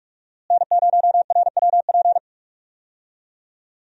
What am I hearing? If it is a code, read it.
Morse code, heard as a keyed tone, about 43 wpm, 700 Hz. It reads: N0RWP
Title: USA Call Signs - 5 Characters - Repeated 2x - 45wpm